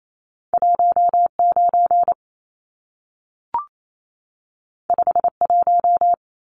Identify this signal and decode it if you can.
Morse code, heard as a keyed tone, about 28 wpm, 700 Hz. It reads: 19 51
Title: Numbers - 2-Digit 28wpm